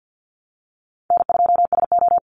Morse code, heard as a keyed tone, about 50 wpm, 700 Hz. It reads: N2SO
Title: Call Signs - 50wpm